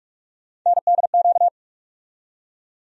Morse code, heard as a keyed tone, about 46 wpm, 700 Hz. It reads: NDQ
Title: USA Call Sign Postfixes - 3 Letters 45wpm